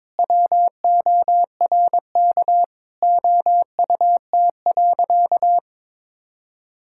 Morse code, heard as a keyed tone, about 21 wpm, 700 Hz. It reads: WORK OUT.
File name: practice / 22wpm / Sentences with 3-Word Phrases - 22wpm